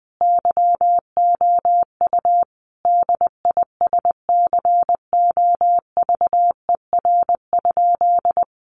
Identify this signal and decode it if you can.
Morse code, heard as a keyed tone, about 20 wpm, 700 Hz. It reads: YOU DISCOVER?
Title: Sentences from Top 1100 Words 20wpm